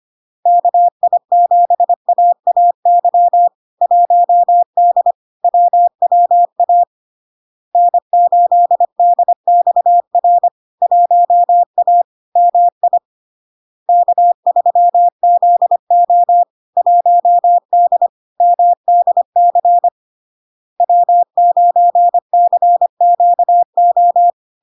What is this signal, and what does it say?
Morse code, heard as a keyed tone, about 25 wpm, 700 Hz. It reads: KI7AAY 1D WWA N8DXR 1A MI K3ZO 1D MDC W9CQO
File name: LARK-FD-2022-Log-in-CW-25wpm-700-Hz